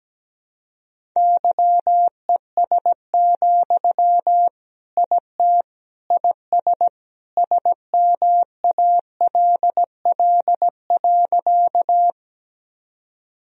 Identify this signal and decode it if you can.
Morse code, heard as a keyed tone, about 17 wpm, 700 Hz. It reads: YES, IT IS SMALL.
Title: Sentences from Top 1000 Words - Review Part 1 - 17wpm